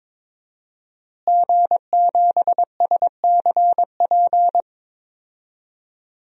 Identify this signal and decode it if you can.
Morse code, heard as a keyed tone, about 22 wpm, 700 Hz. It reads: G7SCP
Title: DXCC-100 Call Signs - 5 Characters - Repeated 2x - 22wpm